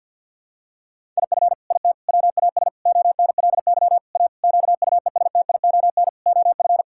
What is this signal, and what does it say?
Morse code, heard as a keyed tone, about 50 wpm, 700 Hz. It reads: IF IT WAS ONLY A QUESTION OF
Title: Sentences from Top 1600 Words 50wpm